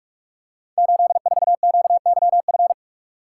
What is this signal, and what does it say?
Morse code, heard as a keyed tone, about 44 wpm, 700 Hz. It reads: 84QYF
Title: Code Groups - 5 Letter:Numbers - 45wpm